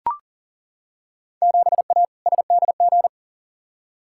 Morse code, heard as a keyed tone, about 40 wpm, 700 Hz. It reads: 7A SDG
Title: Field Day Exchanges Encore - 40wpm